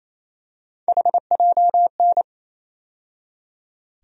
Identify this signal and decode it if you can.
Morse code, heard as a keyed tone, about 28 wpm, 700 Hz. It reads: HJN
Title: USA Call Sign Postfixes - 3 Letters 28wpm